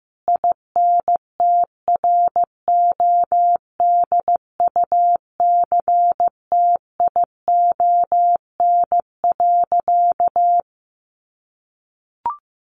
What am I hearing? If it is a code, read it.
Morse code, heard as a keyed tone, about 15 wpm, 700 Hz. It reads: INTRODUCTION.
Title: Sentences from Top 1000 Words 15wpm